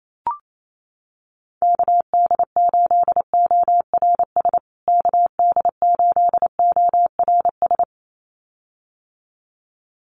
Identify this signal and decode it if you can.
Morse code, heard as a keyed tone, about 28 wpm, 700 Hz. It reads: KD8ORS KD8ORS
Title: USA Call Signs Repeated 2x 28wpm